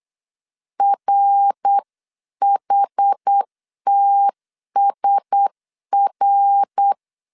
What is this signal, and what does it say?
Morse code, heard as a keyed tone, about 8 wpm, 800 Hz. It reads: RHTSR
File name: CW / DET / 1 6 1500C S C 8wpm 5219